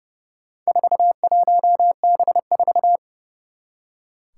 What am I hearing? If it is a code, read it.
Morse code, heard as a keyed tone, about 30 wpm, 700 Hz. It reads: 41B4
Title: Four Challenging Letter-Number Combinations 30wpm